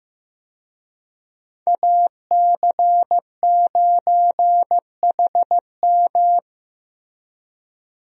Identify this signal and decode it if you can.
Morse code, heard as a keyed tone, about 15 wpm, 700 Hz. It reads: AC9HM
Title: USA Call Signs - 5 Characters - 15wpm